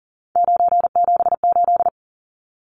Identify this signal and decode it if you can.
Morse code, heard as a keyed tone, about 40 wpm, 700 Hz. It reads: 978
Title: Numbers - 3-Digit 40wpm